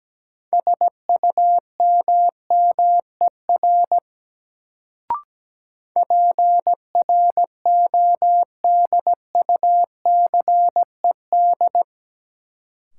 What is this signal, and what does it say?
Morse code, heard as a keyed tone, about 17 wpm, 700 Hz. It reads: SUMMER PRODUCED
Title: Sentences from Top 1000 Words - Review Part 6 - New Word Drill - 17wpm